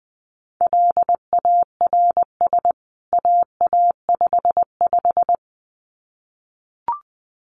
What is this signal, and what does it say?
Morse code, heard as a keyed tone, about 20 wpm, 700 Hz. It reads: LARS AA55
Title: CWT Exchange 20wpm